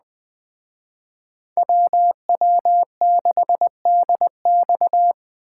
Morse code, heard as a keyed tone, about 20 wpm, 700 Hz. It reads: WW6DX
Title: Speed Racing - USA 2x2 Call Signs 17wpm